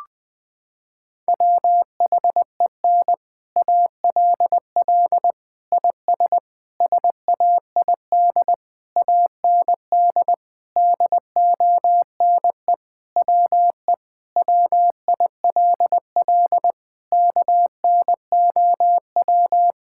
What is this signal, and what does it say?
Morse code, heard as a keyed tone, about 20 wpm, 700 Hz. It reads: WHEN ALL IS SAID AND DONE WE WILL KNOW
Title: Sentences with 6-Word Phrases - 20wpm